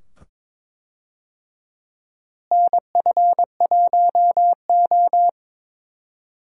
Morse code, heard as a keyed tone, about 22 wpm, 700 Hz. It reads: NF1O
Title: USA Call Signs - 4 Characters - Repeated 2x - 22wpm